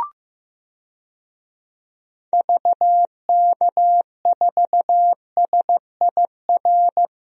Morse code, heard as a keyed tone, about 15 wpm, 700 Hz. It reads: VK4SIR
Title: DXCC-100 Call Signs - 6 Characters - 15wpm